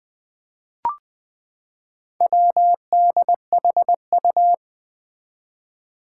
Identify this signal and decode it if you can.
Morse code, heard as a keyed tone, about 21 wpm, 700 Hz. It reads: WDHU
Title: Four Challenging Letter-Number Combinations 20wpm